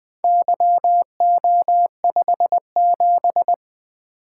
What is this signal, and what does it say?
Morse code, heard as a keyed tone, about 20 wpm, 700 Hz. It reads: YO57
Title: Four Letter-Number Combinations - 20wpm